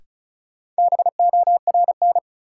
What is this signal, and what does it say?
Morse code, heard as a keyed tone, about 36 wpm, 700 Hz. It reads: BORN
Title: Sentences from Top 1000 Words - New Word Drill - 35wpm